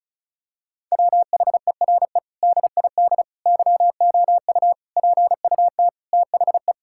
Morse code, heard as a keyed tone, about 36 wpm, 700 Hz. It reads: WHERE DID YOU PUT THE
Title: Sentences from Top 1600 Words 35wpm